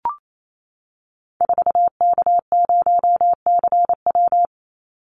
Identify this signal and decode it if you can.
Morse code, heard as a keyed tone, about 28 wpm, 700 Hz. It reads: 4K0CW
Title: Call Signs Repeated 3x - 28wpm